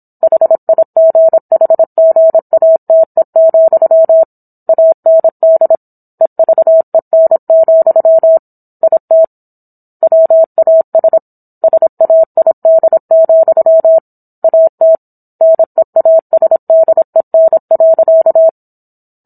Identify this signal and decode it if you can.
Morse code, heard as a keyed tone, about 27 wpm, 650 Hz. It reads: HIGHGATE, AND EVEN, IT WAS SAID, AT NEASDEN.
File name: Book One / 17-Chapter 17 - 26 WPM